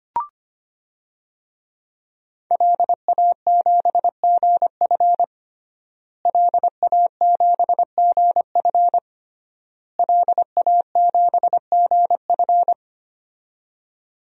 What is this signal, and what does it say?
Morse code, heard as a keyed tone, about 25 wpm, 700 Hz. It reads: LA7GF LA7GF LA7GF
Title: DXCC-100 Call Signs - 5 Characters - Repeated 3x - 25wpm